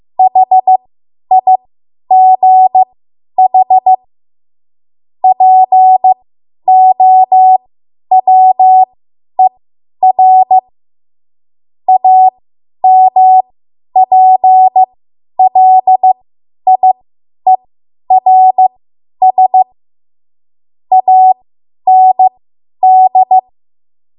Morse code, heard as a keyed tone, about 15 wpm, spaced as for 10 wpm, 750 Hz. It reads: HIGH POWER AMPLIERS AND
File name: Morse / 10WPM